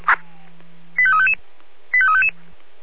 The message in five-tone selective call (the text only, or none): 85355 85355